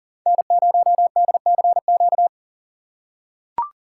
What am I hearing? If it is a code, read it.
Morse code, heard as a keyed tone, about 40 wpm, 700 Hz. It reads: N0DCQ
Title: USA Call Signs - 5 Characters - 6x Character Spacing - 40wpm